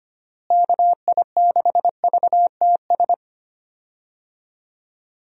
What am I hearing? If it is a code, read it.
Morse code, heard as a keyed tone, about 25 wpm, 700 Hz. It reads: KI6VTS
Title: USA Call Signs - 6 Characters - Repeated 2x - 25wpm